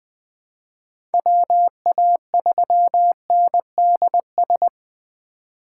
Morse code, heard as a keyed tone, about 20 wpm, 700 Hz. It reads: WA3NDS
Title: USA Call Signs - 6 Characters - 4x Character Spacing - 20wpm